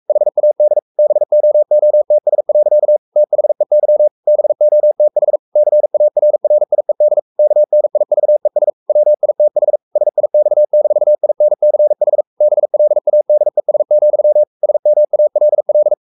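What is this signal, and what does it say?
Morse code, heard as a keyed tone, about 44 wpm, 600 Hz. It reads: HAD BOOTS. THEY BOTH CARRIED KNIVES WITH SIX-INCH BLADES, SMALL
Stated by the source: Morse code at 43 wpm sending Ken Follett's novel Never in English, from CW med Gnister.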